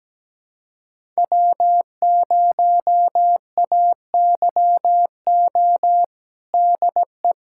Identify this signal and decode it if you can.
Morse code, heard as a keyed tone, about 17 wpm, 700 Hz. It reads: W0AYO DE
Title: USA Callsign DE Callsign - 5 Characters - 17wpm